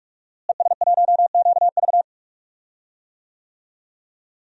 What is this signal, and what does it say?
Morse code, heard as a keyed tone, about 41 wpm, 700 Hz. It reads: ES1QV
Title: DXCC-100 Call Signs - 5 Characters - Repeated 3x - 45wpm